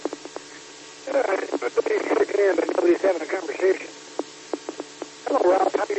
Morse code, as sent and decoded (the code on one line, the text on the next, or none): none